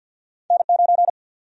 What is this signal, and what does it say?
Morse code, heard as a keyed tone, about 53 wpm, 700 Hz. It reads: N9
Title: USA Call Sign Prefixes - Repeated 2x - 50wpm